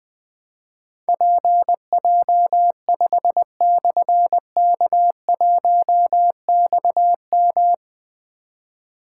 Morse code, heard as a keyed tone, about 20 wpm, 700 Hz. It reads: PJ5/K1XM
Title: Difficult Call Signs 20wpm